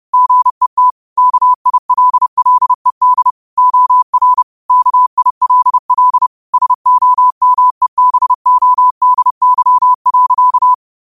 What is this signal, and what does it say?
Morse code, heard as a keyed tone, about 30 wpm, 1000 Hz. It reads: GET KILLED OR KILL SOMEBODY.
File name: qotdfiles / 30wpm